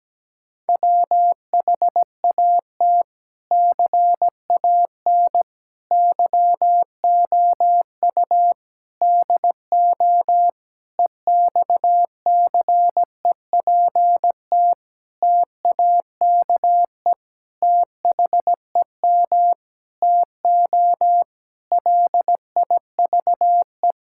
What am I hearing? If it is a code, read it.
Morse code, heard as a keyed tone, about 17 wpm, 700 Hz. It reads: WHAT CAN YOU DO EXCEPT TAKE THEM TO LIVE
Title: Sentences from Top 1600 Words 17wpm